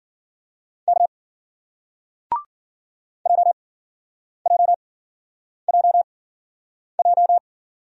Morse code, heard as a keyed tone, about 55 wpm, spaced as for 9 wpm, 700 Hz. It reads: K JJJJ